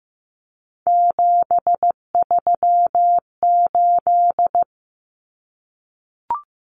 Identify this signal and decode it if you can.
Morse code, heard as a keyed tone, about 15 wpm, 700 Hz. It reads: 738
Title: Common Cut Numbers - 3-Digit 15wpm